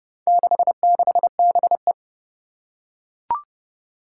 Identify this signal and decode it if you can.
Morse code, heard as a keyed tone, about 30 wpm, 700 Hz. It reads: 66BE
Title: Four Challenging Letter-Number Combinations 30wpm